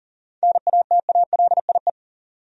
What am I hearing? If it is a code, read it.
Morse code, heard as a keyed tone, about 40 wpm, 700 Hz. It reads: NATALIE
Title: USA Common Names - 40wpm